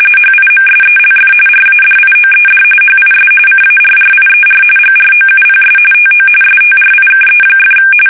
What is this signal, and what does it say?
RTTY, about 45.45 baud. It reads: HIMSELF RADIOED HIS CARRIER GROUPS TO STRIKE!
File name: fsk850 / battleshipnj